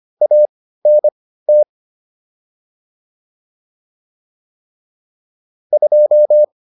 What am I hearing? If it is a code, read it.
Morse code, heard as a keyed tone, about 25 wpm, spaced as for 15 wpm, 600 Hz. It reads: ANT 2